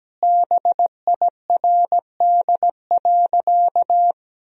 Morse code, heard as a keyed tone, about 17 wpm, 700 Hz. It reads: BIRD.
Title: Sentences from Top 1000 Words - Review Part 2 - 17wpm